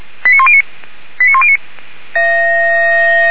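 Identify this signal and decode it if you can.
Five-tone selective call (ZVEI), heard as quick runs of five tones, tones of about 70 ms. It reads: 89109 89109
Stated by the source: ZVEI five-tone selective call received on SDR.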